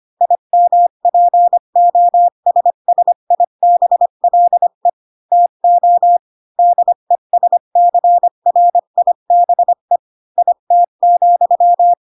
Morse code, heard as a keyed tone, about 25 wpm, 700 Hz. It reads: IMPOSSIBLE TO DESCRIBE IT,
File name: Pinocchio / Pinocchio ch5 20 25 ik1hge